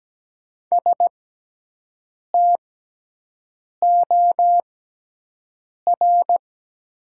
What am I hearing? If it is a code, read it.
Morse code, heard as a keyed tone, about 17 wpm, spaced as for 6 wpm, 700 Hz. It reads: STOR
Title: 6x Character Spacing - Top 500 Words - 17wpm